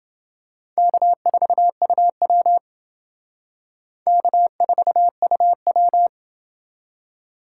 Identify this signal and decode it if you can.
Morse code, heard as a keyed tone, about 29 wpm, 700 Hz. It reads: K4UW K4UW